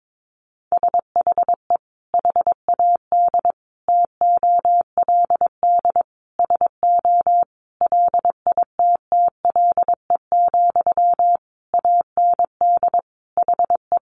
Morse code, heard as a keyed tone, about 22 wpm, 700 Hz. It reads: SHE HAD TOLD SO LITTLE, AND HE
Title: Sentences from Top 1500 Words 22wpm